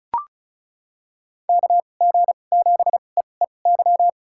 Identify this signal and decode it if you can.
Morse code, heard as a keyed tone, about 34 wpm, spaced as for 25 wpm, 700 Hz. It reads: KG7EEY
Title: USA Call Signs - 6 Characters - 2x Character Spacing - 35wpm